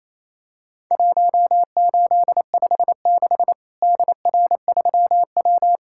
Morse code, heard as a keyed tone, about 28 wpm, 700 Hz. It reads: 1856 DR3W
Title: Worked All Europe DX Contest - QTC Exchange - 28wpm